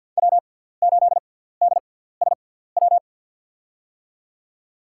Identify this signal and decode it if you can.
Morse code, heard as a keyed tone, about 50 wpm, spaced as for 17 wpm, 700 Hz. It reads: W8DSW
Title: USA Call Signs - 5 Characters - 6x Character Spacing - 50wpm